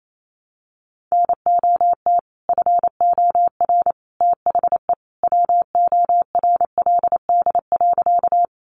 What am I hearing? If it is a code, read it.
Morse code, heard as a keyed tone, about 28 wpm, 700 Hz. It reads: NOT FOR THE WORLD.
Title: Sentences from Top 600 Words 28wpm